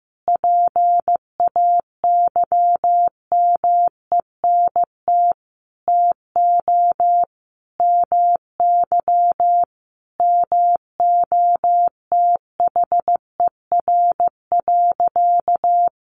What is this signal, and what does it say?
Morse code, heard as a keyed tone, about 15 wpm, 700 Hz. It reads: PAYMENT TO MY MOTHER.
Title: Sentences from Top 1400 Words 15wpm